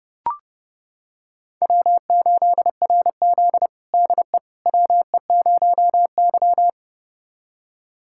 Morse code, heard as a keyed tone, about 30 wpm, 700 Hz. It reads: W8RZ DE WE0Y
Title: USA Callsign DE Callsign - 4 Characters - 30wpm